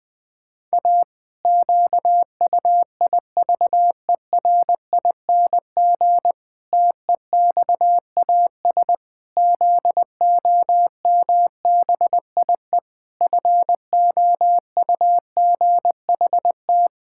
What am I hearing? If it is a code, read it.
Morse code, heard as a keyed tone, about 20 wpm, 700 Hz. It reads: A QUIVERING TEXAS ZOMBIE FOUGHT
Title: Pangram Challenge v2 - 20wpm